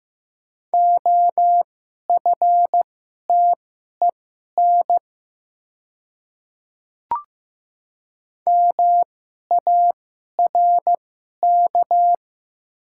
Morse code, heard as a keyed tone, about 15 wpm, spaced as for 11 wpm, 700 Hz. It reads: OFTEN MARK
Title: 2x Character Spacing - Top 300 Words - 15wpm